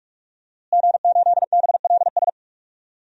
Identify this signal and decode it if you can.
Morse code, heard as a keyed tone, about 46 wpm, 700 Hz. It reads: G8BLS
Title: DXCC-100 Call Signs - 5 Characters - Repeated 2x - 45wpm